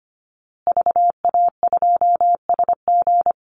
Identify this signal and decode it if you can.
Morse code, heard as a keyed tone, about 25 wpm, 700 Hz. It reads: VA2SG
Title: Call Signs Repeated 2x - 25wpm